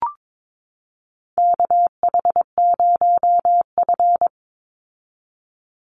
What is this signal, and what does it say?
Morse code, heard as a keyed tone, about 22 wpm, 700 Hz. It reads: KH0F
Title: USA Call Signs 22wpm